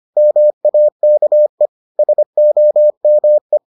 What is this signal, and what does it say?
Morse code, heard as a keyed tone, about 25 wpm, 600 Hz. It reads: MAKE SOME